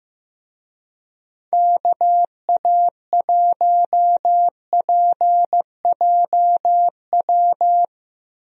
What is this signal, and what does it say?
Morse code, heard as a keyed tone, about 15 wpm, 700 Hz. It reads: KA1PJW
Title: USA Call Signs - 6 Characters - Repeated 3x - 15wpm